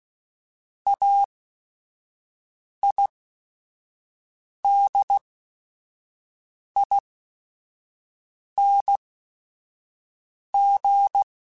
Morse code, heard as a keyed tone, about 16 wpm, spaced as for 5 wpm, 800 Hz. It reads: AIDING
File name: Special INGs / Special INGs-mp3s / Sngl Ing @5